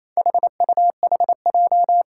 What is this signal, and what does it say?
Morse code, heard as a keyed tone, about 27 wpm, 700 Hz. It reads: HUHJ
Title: Four Challenging Letter-Number Combinations 28wpm